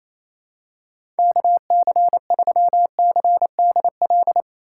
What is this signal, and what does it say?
Morse code, heard as a keyed tone, about 28 wpm, 700 Hz. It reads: KC3CDL
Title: USA Call Signs - 6 Characters - 3x Character Spacing - 28wpm